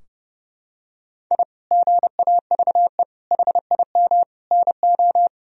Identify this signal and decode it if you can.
Morse code, heard as a keyed tone, about 30 wpm, 700 Hz. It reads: I GAVE HIM NO
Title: Sentences from Top 1000 Words - Review Part 2 - 30wpm